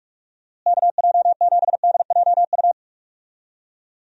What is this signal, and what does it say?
Morse code, heard as a keyed tone, about 45 wpm, 700 Hz. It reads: KJ7DJU
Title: USA Call Signs - 6 Characters - Repeated 3x - 45wpm